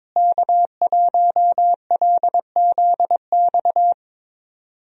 Morse code, heard as a keyed tone, about 22 wpm, 700 Hz. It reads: K1LZX
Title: USA Call Signs - 5 Characters - Repeated 2x - 22wpm